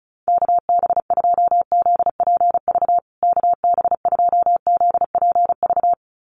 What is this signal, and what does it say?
Morse code, heard as a keyed tone, about 34 wpm, 700 Hz. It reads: KB2ZPV KB2ZPV
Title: USA Call Signs Repeated 2x 35wpm